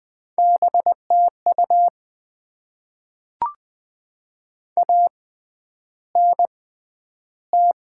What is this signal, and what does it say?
Morse code, heard as a keyed tone, about 20 wpm, spaced as for 10 wpm, 700 Hz. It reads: BTU ANT
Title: Top 20 QSO Elements - 6x Character Spacing - 20wpm